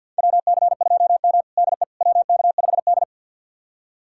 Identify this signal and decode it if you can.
Morse code, heard as a keyed tone, about 52 wpm, 700 Hz. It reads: WC2M DE WK5D
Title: USA Callsign DE Callsign - 4 Characters - 50wpm